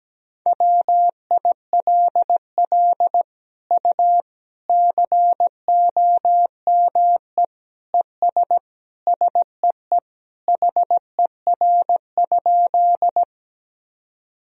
Morse code, heard as a keyed tone, about 17 wpm, 700 Hz. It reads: WILL U COME ES SEE HER?
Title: Sentences from Top 100 Words and Abbreviations 17wpm